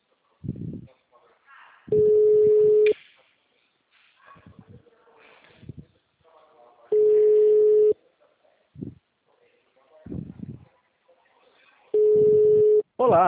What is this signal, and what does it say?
Ringback tone (Europe, 425 Hz), with speech.
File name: out- -2022- - - .31139